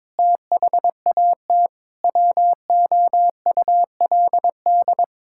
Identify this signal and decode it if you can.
Morse code, heard as a keyed tone, about 22 wpm, 700 Hz. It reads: THAT WOULD
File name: Sets of 9 Words - 22wpm